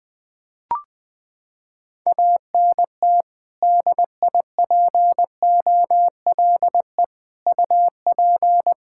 Morse code, heard as a keyed tone, about 20 wpm, 700 Hz. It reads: ANT DIPOLE UP
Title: Practice QSO - Exchange 2 - Info - 20wpm